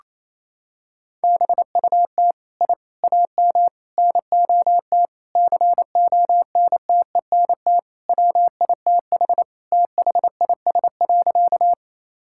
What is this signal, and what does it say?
Morse code, heard as a keyed tone, about 28 wpm, 700 Hz. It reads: BUT I AM NOT CONTENT WITH THIS.
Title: Sentences from Top 1000 Words - Review Part 6 - 28wpm